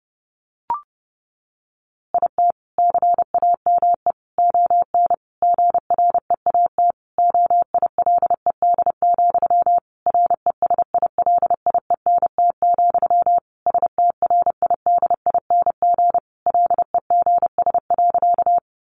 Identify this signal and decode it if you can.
Morse code, heard as a keyed tone, about 30 wpm, 700 Hz. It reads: IT CAME ON GREAT OILED, RESILIENT, STRIDING LEGS.
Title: A Sound Of Thunder 30wpm